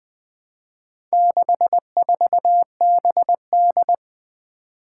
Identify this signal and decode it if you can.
Morse code, heard as a keyed tone, about 21 wpm, 700 Hz. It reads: 64BD
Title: Four Challenging Letter-Number Combinations 20wpm